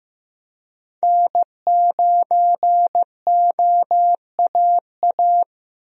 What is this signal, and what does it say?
Morse code, heard as a keyed tone, about 15 wpm, 700 Hz. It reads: N9OAA
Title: USA Call Signs - 5 Characters - Repeated 2x - 15wpm